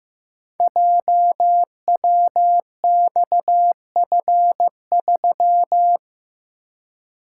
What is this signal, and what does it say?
Morse code, heard as a keyed tone, about 15 wpm, 700 Hz. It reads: JWXF3
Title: Code Groups - 5 Letter:Numbers - 15wpm